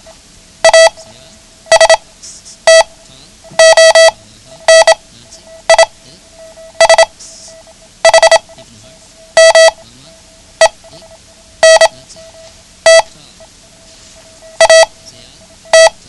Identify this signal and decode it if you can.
Morse code, heard as a keyed tone, about 26 wpm, spaced as for 9 wpm, 650 Hz. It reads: ASTONISHMENT AT
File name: ham radio / CodeQuick / Code Quick 09